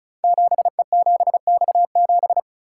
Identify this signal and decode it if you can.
Morse code, heard as a keyed tone, about 36 wpm, 690 Hz. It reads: 7E7X7
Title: Code Groups - 5 Letter:Numbers - 35wpm